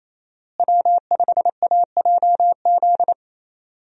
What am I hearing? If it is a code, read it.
Morse code, heard as a keyed tone, about 28 wpm, 700 Hz. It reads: W5AJZ
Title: USA Call Signs - 5 Characters - Repeated 3x - 28wpm